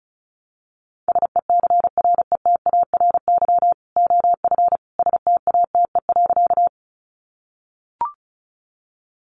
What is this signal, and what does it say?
Morse code, heard as a keyed tone, about 34 wpm, 700 Hz. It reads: SECRETARY OF STATE.
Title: Sentences from Top 1300 Words 35wpm